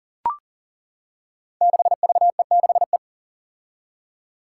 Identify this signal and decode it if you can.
Morse code, heard as a keyed tone, about 40 wpm, 700 Hz. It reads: 6VE6E
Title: Five Challenging Letter-Number Combinations 40wpm